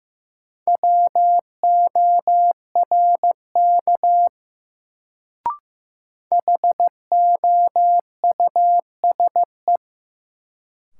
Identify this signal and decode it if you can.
Morse code, heard as a keyed tone, about 16 wpm, 700 Hz. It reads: WORK HOUSE
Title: Sentences from Top 200 Words - New Word Drill - 15wpm